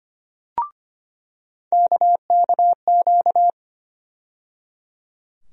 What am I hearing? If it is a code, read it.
Morse code, heard as a keyed tone, about 25 wpm, 700 Hz. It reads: KKQ
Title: USA Call Sign Postfixes - 3 Letters 25wpm